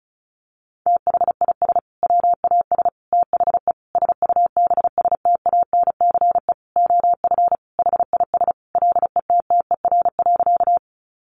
Morse code, heard as a keyed tone, about 34 wpm, 700 Hz. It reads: THIS WAS THE SUBSTANCE OF HIS LETTER.
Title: Sentences from Top 1500 Words 35wpm